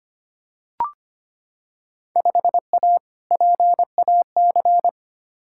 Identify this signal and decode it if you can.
Morse code, heard as a keyed tone, about 25 wpm, 700 Hz. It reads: 5A PAC
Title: Field Day Exchanges Encore - 25wpm